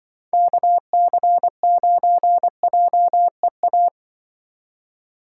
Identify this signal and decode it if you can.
Morse code, heard as a keyed tone, about 24 wpm, 700 Hz. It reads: KC9JEA
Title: Speed Racing - USA 2x3 Call Signs 20wpm